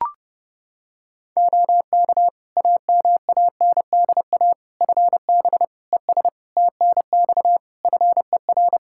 Morse code, heard as a keyed tone, about 30 wpm, 700 Hz. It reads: OK AMANDA FB ES TNX FER
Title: K1USN SST QSO Practice 30wpm